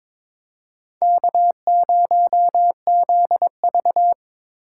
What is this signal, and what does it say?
Morse code, heard as a keyed tone, about 22 wpm, 700 Hz. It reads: K0ZV
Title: USA Call Signs - 3x Character Spacing - 22wpm